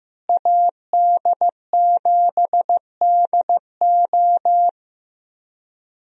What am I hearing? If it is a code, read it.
Morse code, heard as a keyed tone, about 15 wpm, 700 Hz. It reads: AD7DO
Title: USA Call Signs - 5 Characters - Repeated 3x - 15wpm